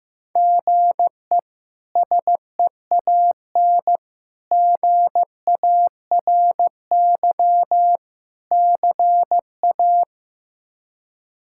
Morse code, heard as a keyed tone, about 16 wpm, 700 Hz. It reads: GE SEAN GARY CA
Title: K1USN SST QSO Practice - 15wpm